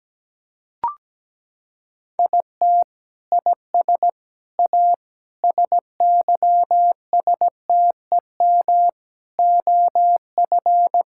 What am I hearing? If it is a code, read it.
Morse code, heard as a keyed tone, about 17 wpm, 700 Hz. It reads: IT IS A SYSTEM OF
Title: Sentences from Top 1000 Words - Review Part 6 - 17wpm